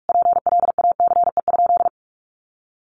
Morse code, heard as a keyed tone, about 44 wpm, 700 Hz. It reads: PLACE?
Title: Sentences from Top 200 Words 45wpm